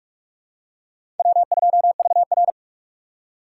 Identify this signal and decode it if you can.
Morse code, heard as a keyed tone, about 44 wpm, 700 Hz. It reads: W2VR
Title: USA Call Signs - 4 Characters - 6x Character Spacing - 45wpm